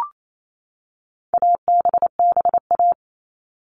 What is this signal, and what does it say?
Morse code, heard as a keyed tone, about 28 wpm, 700 Hz. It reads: ABBA
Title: All Cut Numbers - 4-Digit 28wpm